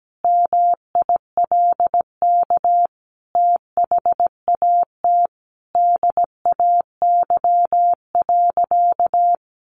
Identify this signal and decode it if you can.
Morse code, heard as a keyed tone, about 17 wpm, 700 Hz. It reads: MILK THAT DAY.
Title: Sentences from Top 1500 Words 17wpm